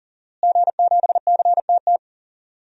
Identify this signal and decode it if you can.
Morse code, heard as a keyed tone, about 37 wpm, 700 Hz. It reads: G7CTT
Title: DXCC-100 Call Signs - 5 Characters - Repeated 3x - 40wpm